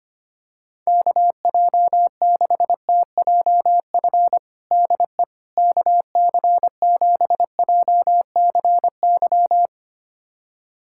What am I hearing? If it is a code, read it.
Morse code, heard as a keyed tone, about 25 wpm, 700 Hz. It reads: KJ6TJF DE KC7JCY
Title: USA Callsign DE Callsign - 6 Characters - 25wpm